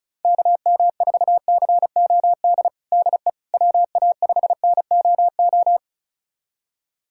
Morse code, heard as a keyed tone, about 36 wpm, 700 Hz. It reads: KM4COD DE WA5NOO
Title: USA Callsign DE Callsign - 6 Characters - 35wpm